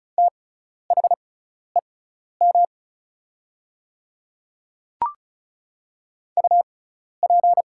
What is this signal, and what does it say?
Morse code, heard as a keyed tone, about 36 wpm, spaced as for 12 wpm, 700 Hz. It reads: THEM UP
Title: 6x Character Spacing - Top 100 Words - 35wpm